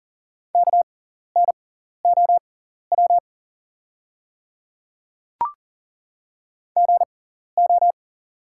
Morse code, heard as a keyed tone, about 39 wpm, spaced as for 14 wpm, 690 Hz. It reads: KNOW GO
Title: 6x Character Spacing - Top 100 Words - 40wpm